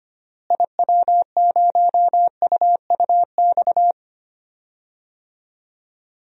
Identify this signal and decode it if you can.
Morse code, heard as a keyed tone, about 25 wpm, 700 Hz. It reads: IW0UUX
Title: DXCC-100 Call Signs - 6 Characters - Repeated 2x - 25wpm